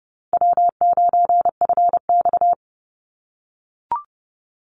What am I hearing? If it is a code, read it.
Morse code, heard as a keyed tone, about 30 wpm, 700 Hz. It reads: W9FX
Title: Easy Call Signs - 30wpm